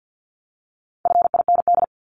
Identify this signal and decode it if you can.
Morse code, heard as a keyed tone, about 50 wpm, 700 Hz. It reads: FIND
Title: Top 500 Words 50wpm